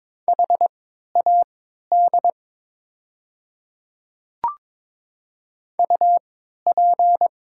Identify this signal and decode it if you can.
Morse code, heard as a keyed tone, about 22 wpm, spaced as for 12 wpm, 700 Hz. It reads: HAD UP
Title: 3x Character Spacing - Top 100 Words - 22wpm